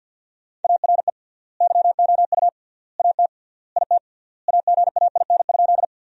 Morse code, heard as a keyed tone, about 50 wpm, 700 Hz. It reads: ARE YOU AT IT AGAIN?
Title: 3x Word-Spacing - Sentences from Top 200 Words - 50wpm